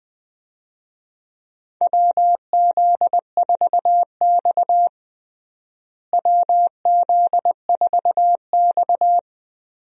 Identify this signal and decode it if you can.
Morse code, heard as a keyed tone, about 20 wpm, 700 Hz. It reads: WZ4X WZ4X
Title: USA Call Signs - 4 Characters - Repeated 3x - 20wpm